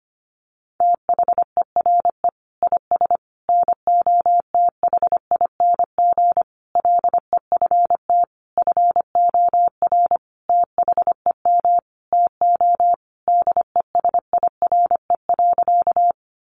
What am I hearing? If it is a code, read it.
Morse code, heard as a keyed tone, about 25 wpm, 700 Hz. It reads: THERE IS NOTHING LEFT FOR THEM TO DESIRE.
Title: Sentences from Top 1100 Words 25wpm